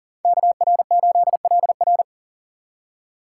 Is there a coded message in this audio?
Yes: Morse code from 0.2 to 2.0 s, heard as a keyed tone, about 50 wpm, 700 Hz. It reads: KR8LR